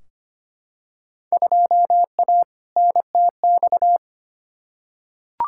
Morse code, heard as a keyed tone, about 25 wpm, 700 Hz. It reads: 2A NTX
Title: Field Day Exchanges - 25wpm